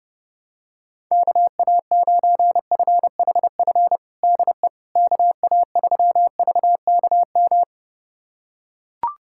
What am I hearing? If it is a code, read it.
Morse code, heard as a keyed tone, about 28 wpm, 700 Hz. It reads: KA9FHF DE KA3VKM
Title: USA Callsign DE Callsign - 6 Characters - 30wpm